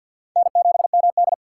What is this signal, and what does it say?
Morse code, heard as a keyed tone, about 50 wpm, 700 Hz. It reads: N7MD
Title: USA Call Signs - 4 Characters - 3x Character Spacing - 50wpm